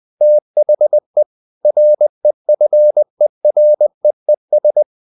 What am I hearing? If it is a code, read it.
Morse code, heard as a keyed tone, about 20 wpm, 600 Hz. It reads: THE REFEREES
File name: the old man and the sea part 2 20wpm